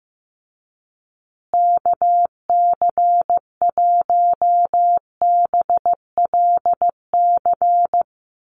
Morse code, heard as a keyed tone, about 15 wpm, 700 Hz. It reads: KC1BLC
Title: USA Call Signs Repeated 3x 15wpm